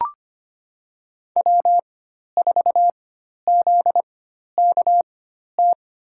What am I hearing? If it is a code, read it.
Morse code, heard as a keyed tone, about 25 wpm, spaced as for 12 wpm, 700 Hz. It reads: W4ZKT
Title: USA Call Signs - 5 Characters - 4x Character Spacing - 25wpm